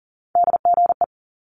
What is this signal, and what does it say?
Morse code, heard as a keyed tone, about 39 wpm, 700 Hz. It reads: DGE
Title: Multi-letter Phonemes Quick Fire 40wpm